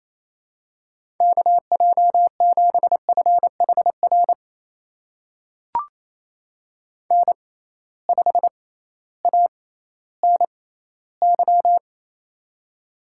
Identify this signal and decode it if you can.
Morse code, heard as a keyed tone, about 28 wpm, spaced as for 15 wpm, 700 Hz. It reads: KJ7FHR N5ANY
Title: USA Call Signs - 6x Character Spacing - 28wpm